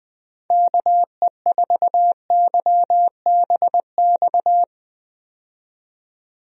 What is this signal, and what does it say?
Morse code, heard as a keyed tone, about 21 wpm, 700 Hz. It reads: KE4YBX
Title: USA Call Signs - 6 Characters - 20wpm